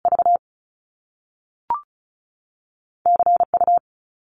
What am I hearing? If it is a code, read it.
Morse code, heard as a keyed tone, about 36 wpm, 700 Hz. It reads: V C U
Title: Top Two Letter Combinations in Words 35wpm